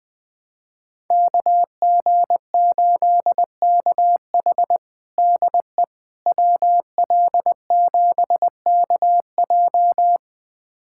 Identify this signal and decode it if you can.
Morse code, heard as a keyed tone, about 20 wpm, 700 Hz. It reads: KG8KH DE WL7KJ
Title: USA Callsign DE Callsign - 5 Characters - 20wpm